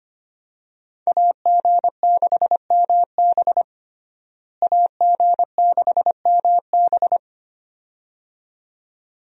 Morse code, heard as a keyed tone, about 25 wpm, 700 Hz. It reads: AG6MB AG6MB
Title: USA Call Signs - 5 Characters - Repeated 2x - 25wpm